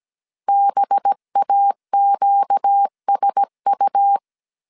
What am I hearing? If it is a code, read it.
Morse code, heard as a keyed tone, about 17 wpm, 800 Hz. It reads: BAQSU
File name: CW / DET / 1 6 1500C S A 16wpm 2635